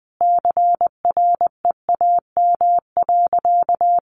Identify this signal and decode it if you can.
Morse code, heard as a keyed tone, about 20 wpm, 700 Hz. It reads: CREAM.
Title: Sentences from Top 1500 Words 20wpm